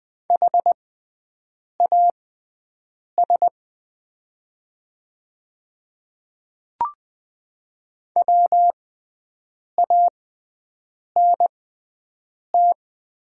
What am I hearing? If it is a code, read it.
Morse code, heard as a keyed tone, about 20 wpm, spaced as for 7 wpm, 700 Hz. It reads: HAS WANT
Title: 6x Character Spacing - Top 200 Words - 20wpm